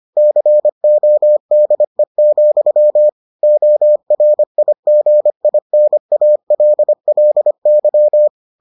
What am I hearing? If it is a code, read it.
Morse code, heard as a keyed tone, about 25 wpm, 600 Hz. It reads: CODE, ORIGINALLY